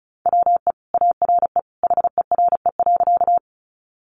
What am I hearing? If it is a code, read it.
Morse code, heard as a keyed tone, about 34 wpm, 700 Hz. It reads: WE ARE HERE.
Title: Sentences from Top 200 Words 35wpm